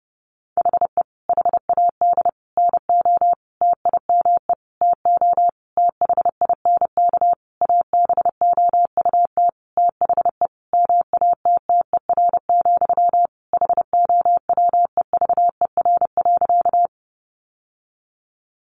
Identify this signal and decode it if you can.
Morse code, heard as a keyed tone, about 30 wpm, 700 Hz. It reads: HE HAD NO TIME TO THINK ABOUT THE MATTER, HOWEVER.
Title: Sentences from Top 700 Words 30wpm